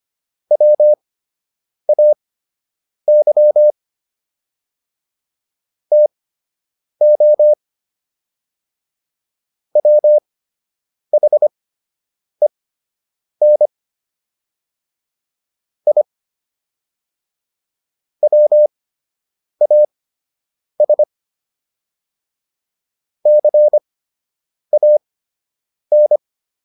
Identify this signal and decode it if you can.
Morse code, heard as a keyed tone, about 25 wpm, spaced as for 8 wpm, 600 Hz. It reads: WAY TO WHEN I WAS CAN